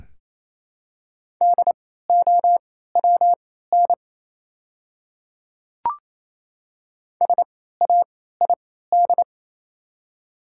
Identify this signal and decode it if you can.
Morse code, heard as a keyed tone, about 28 wpm, spaced as for 16 wpm, 700 Hz. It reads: DOWN SAID
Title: 3x Character Spacing - Top 200 Words - 28wpm